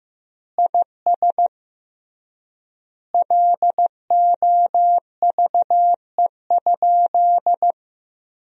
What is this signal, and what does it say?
Morse code, heard as a keyed tone, about 16 wpm, 700 Hz. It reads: IS LOVE?
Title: 3x Word-Spacing - Sentences from Top 400 Words - 15wpm